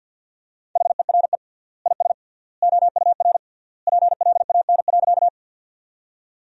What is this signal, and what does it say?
Morse code, heard as a keyed tone, about 49 wpm, 700 Hz. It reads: HERE IS OUR PLAN.
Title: 3x Word-Spacing - Sentences from Top 500 Words - 50wpm